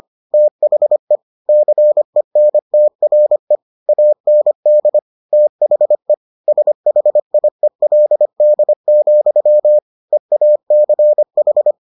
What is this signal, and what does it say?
Morse code, heard as a keyed tone, about 25 wpm, 600 Hz. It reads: THE CENTRE AND THE SHIELD, EACH